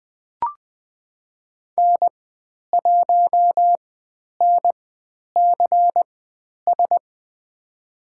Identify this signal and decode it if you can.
Morse code, heard as a keyed tone, about 20 wpm, spaced as for 10 wpm, 700 Hz. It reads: N1NCS
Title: 095 Lesson 16g - Letter c - Call Signs 20wpm